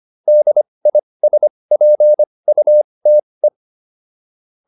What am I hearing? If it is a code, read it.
Morse code, heard as a keyed tone, about 25 wpm, spaced as for 20 wpm, 600 Hz. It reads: DISPUTE
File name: DIS7 20